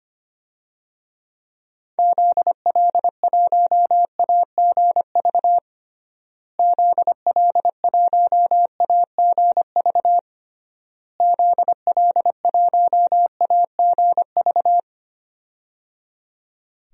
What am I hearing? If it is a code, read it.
Morse code, heard as a keyed tone, about 25 wpm, 700 Hz. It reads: ZL1AGV ZL1AGV ZL1AGV
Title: DXCC-100 Call Signs - 6 Characters - Repeated 3x - 25wpm